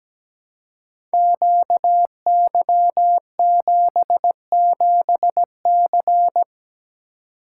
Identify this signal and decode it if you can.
Morse code, heard as a keyed tone, about 17 wpm, 700 Hz. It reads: QY77C
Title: Code Groups - 5 Letter:Numbers - 17wpm